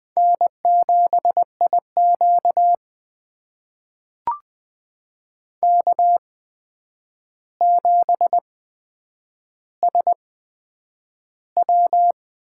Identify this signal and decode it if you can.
Morse code, heard as a keyed tone, about 20 wpm, spaced as for 9 wpm, 700 Hz. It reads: N7IQ K7SW
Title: USA Call Signs - 4 Characters - 8x Character Spacing - 20wpm